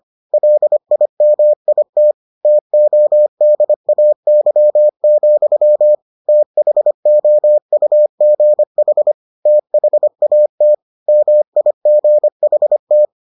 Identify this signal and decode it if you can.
Morse code, heard as a keyed tone, about 25 wpm, 600 Hz. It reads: LIMIT TODAY, THOUGH THAT MIGHT